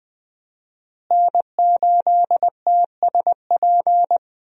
Morse code, heard as a keyed tone, about 20 wpm, 700 Hz. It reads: N8TSP
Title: USA Call Signs - 5 Characters - Repeated 2x - 20wpm